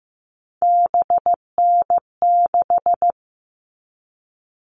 Morse code, heard as a keyed tone, about 15 wpm, 700 Hz. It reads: BN6
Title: All Cut Numbers - 3-Digit 15wpm